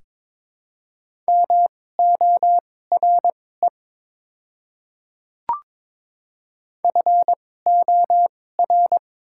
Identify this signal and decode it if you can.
Morse code, heard as a keyed tone, about 22 wpm, spaced as for 16 wpm, 700 Hz. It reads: MORE FOR
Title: 2x Character Spacing - Top 100 Words - 22wpm